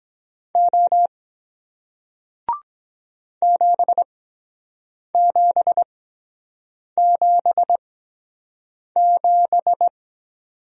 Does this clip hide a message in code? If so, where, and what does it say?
Yes: Morse code from 0.6 to 9.9 s, heard as a keyed tone, about 23 wpm, spaced as for 7 wpm, 700 Hz. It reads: O 7777